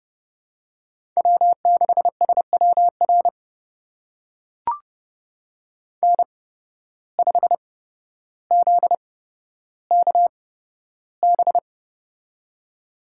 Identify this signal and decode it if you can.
Morse code, heard as a keyed tone, about 30 wpm, spaced as for 13 wpm, 700 Hz. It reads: W6SWR N5ZKB
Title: USA Call Signs - 5 Characters - 8x Character Spacing - 30wpm